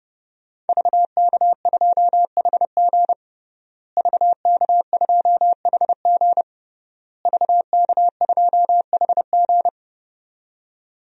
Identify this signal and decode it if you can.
Morse code, heard as a keyed tone, about 30 wpm, 700 Hz. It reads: VK2HG VK2HG VK2HG
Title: DXCC-100 Call Signs - 5 Characters - Repeated 3x - 30wpm